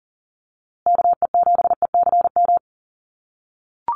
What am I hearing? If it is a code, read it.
Morse code, heard as a keyed tone, about 40 wpm, 700 Hz. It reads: KE7ECM
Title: USA Call Signs Repeated 2x 40wpm